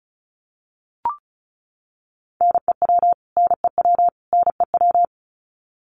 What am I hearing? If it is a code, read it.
Morse code, heard as a keyed tone, about 34 wpm, 700 Hz. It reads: NEW NEW NEW
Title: Top 500 Words Repeated 35wpm